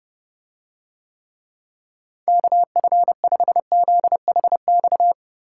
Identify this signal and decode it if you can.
Morse code, heard as a keyed tone, about 30 wpm, 700 Hz. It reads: KF5ZHX